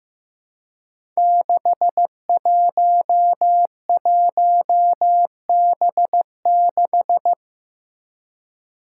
Morse code, heard as a keyed tone, about 15 wpm, 700 Hz. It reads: 611B6
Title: Five Challenging Letter-Number Combinations 15wpm